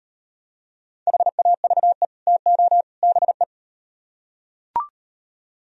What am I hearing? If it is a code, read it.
Morse code, heard as a keyed tone, about 38 wpm, 700 Hz. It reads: HAVE TO BE